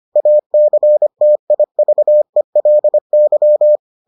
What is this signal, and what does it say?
Morse code, heard as a keyed tone, about 25 wpm, 600 Hz. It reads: ACTIVELY